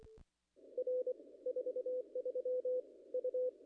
Morse code, heard as a keyed tone, about 24 wpm, 500 Hz. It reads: R 43 U
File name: contests / ss-cw-k5tr-op-n5rz -0000